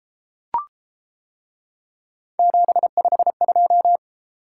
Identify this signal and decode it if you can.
Morse code, heard as a keyed tone, about 32 wpm, 700 Hz. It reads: 752